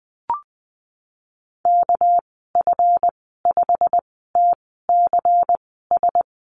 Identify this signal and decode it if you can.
Morse code, heard as a keyed tone, about 20 wpm, spaced as for 15 wpm, 700 Hz. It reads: KF5TCS
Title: 200 Lesson 32g - Letter k - Call Signs 20wpm